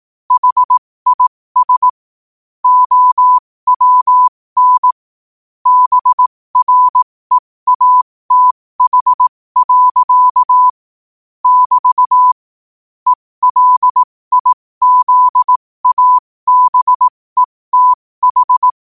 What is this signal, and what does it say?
Morse code, heard as a keyed tone, about 18 wpm, spaced as for 15 wpm, 1000 Hz. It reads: HIS OWN BREATH. <BT> ELIZABETH
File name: qotdfiles / 15wpm